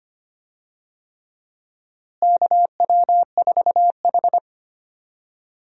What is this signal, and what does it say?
Morse code, heard as a keyed tone, about 25 wpm, 700 Hz. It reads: KW4H
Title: USA Call Signs - 4 Characters - Repeated 3x - 25wpm